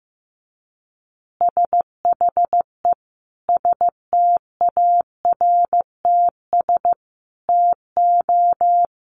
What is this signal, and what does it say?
Morse code, heard as a keyed tone, about 15 wpm, 700 Hz. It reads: SHE STARTS TO
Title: Easy Phrases 15wpm